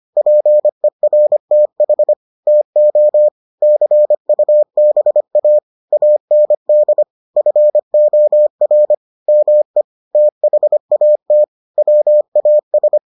Morse code, heard as a keyed tone, about 25 wpm, 600 Hz. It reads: PERTH TO CUBA AND FOR ME THAT WAS